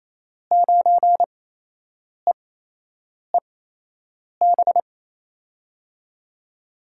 Morse code, heard as a keyed tone, about 27 wpm, spaced as for 8 wpm, 700 Hz. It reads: 9EEB
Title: USA Call Signs - 6 Characters - 8x Character Spacing - 28wpm